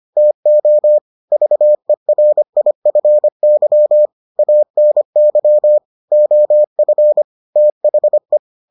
Morse code, heard as a keyed tone, about 25 wpm, 600 Hz. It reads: TO VERIFY ANY OF THE